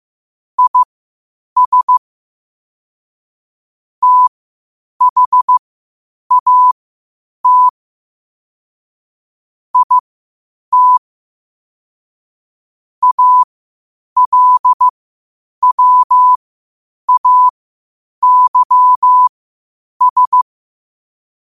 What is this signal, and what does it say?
Morse code, heard as a keyed tone, about 15 wpm, spaced as for 8 wpm, 1000 Hz. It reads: IS THAT IT ALWAYS
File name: qotdfiles / 8wpm